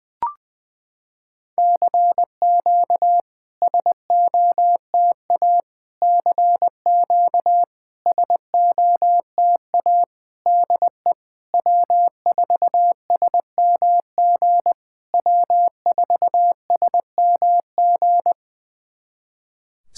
Morse code, heard as a keyed tone, about 20 wpm, 700 Hz. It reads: CQ SOTA CQ SOTA DE W4SMG W4SMG
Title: SOTA QSO Practice 20wpm